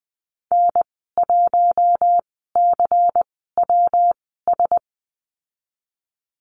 Morse code, heard as a keyed tone, about 20 wpm, spaced as for 15 wpm, 700 Hz. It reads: N1CWS
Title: 122 Lesson 20g - Number 3 - Call Signs 20wpm